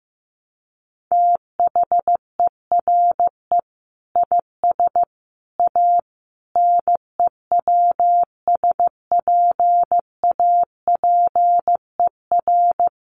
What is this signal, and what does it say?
Morse code, heard as a keyed tone, about 15 wpm, 700 Hz. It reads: THERE IS A NEWSPAPER
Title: Sentences from Top 1100 Words 15wpm